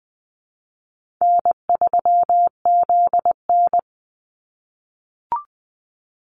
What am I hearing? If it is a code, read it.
Morse code, heard as a keyed tone, about 20 wpm, 700 Hz. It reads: N3ZN
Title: Call Signs - 20wpm